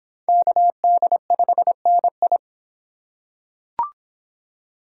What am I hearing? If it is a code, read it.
Morse code, heard as a keyed tone, about 24 wpm, 700 Hz. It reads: KD5NI